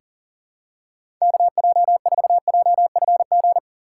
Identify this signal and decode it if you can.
Morse code, heard as a keyed tone, about 40 wpm, 700 Hz. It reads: KJ4JFG
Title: USA Call Signs - 6 Characters - 2x Character Spacing - 40wpm